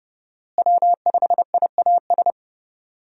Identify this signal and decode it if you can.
Morse code, heard as a keyed tone, about 30 wpm, 700 Hz. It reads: W5IAS
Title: USA Call Signs - 5 Characters - Repeated 3x - 30wpm